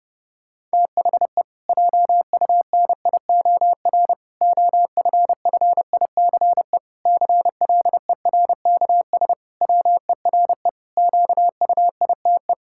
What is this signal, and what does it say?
Morse code, heard as a keyed tone, about 30 wpm, 700 Hz. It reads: THE JUNIOR OFFICE CLERKS WERE QUITE
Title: Pangram Challenge v2 - 30wpm